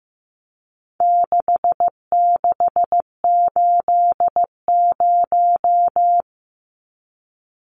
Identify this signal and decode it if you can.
Morse code, heard as a keyed tone, about 15 wpm, 700 Hz. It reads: 6680
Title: Numbers - 4-Digit 15wpm